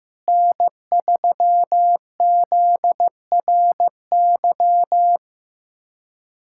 Morse code, heard as a keyed tone, about 15 wpm, 700 Hz. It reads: N3ZRY
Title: USA Callsign DE Callsign 15wpm